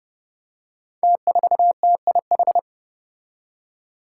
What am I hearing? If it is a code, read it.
Morse code, heard as a keyed tone, about 30 wpm, 700 Hz. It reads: T4TIH
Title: Five Challenging Letter-Number Combinations 30wpm